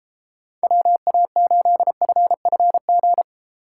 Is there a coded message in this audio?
Yes: Morse code from 0.6 to 3.2 s, heard as a keyed tone, about 32 wpm, 700 Hz. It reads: WA8FFG